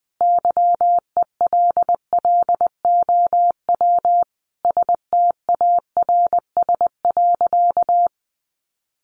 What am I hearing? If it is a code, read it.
Morse code, heard as a keyed tone, about 20 wpm, 700 Hz. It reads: YELLOW STARS.
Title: Additional Easy Phrases 20wpm